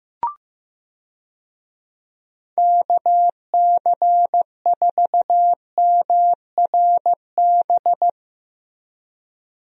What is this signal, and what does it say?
Morse code, heard as a keyed tone, about 15 wpm, 700 Hz. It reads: KC4MRB
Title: USA Call Signs - 6 Characters - Repeated 2x - 15wpm